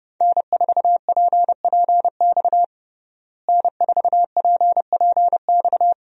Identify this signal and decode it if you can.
Morse code, heard as a keyed tone, about 30 wpm, 700 Hz. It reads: N4PPX N4PPX
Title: USA Call Signs - 5 Characters - Repeated 2x - 30wpm